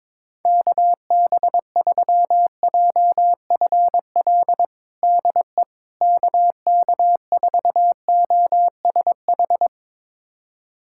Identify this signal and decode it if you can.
Morse code, heard as a keyed tone, about 22 wpm, 700 Hz. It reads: KB3JFL DE KK4OSH
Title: USA Callsign DE Callsign 22wpm